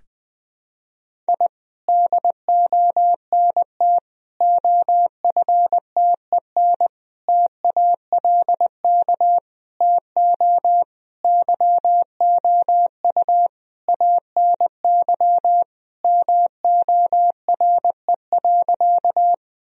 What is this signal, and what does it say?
Morse code, heard as a keyed tone, about 21 wpm, 700 Hz. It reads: I DONT OFTEN TALK TO YOU ANY MORE.
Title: Sentences from Top 1000 Words - Review Part 1 - 20wpm